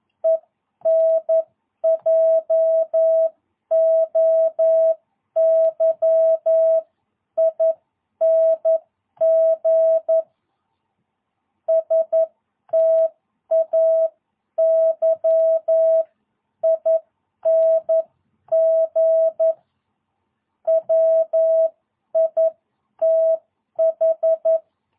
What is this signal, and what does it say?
Morse code, heard as a keyed tone, about 11 wpm, spaced as for 9 wpm, 660 Hz. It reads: ENJOYING STAYING WITH
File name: jh8cbh / 2022cw